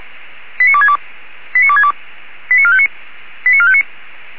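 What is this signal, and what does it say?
Five-tone selective call (ZVEI), heard as quick runs of five tones, tones of about 70 ms. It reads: 89262 89262 89459 89459
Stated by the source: ZVEI five-tone selective call 89269 received on SDR.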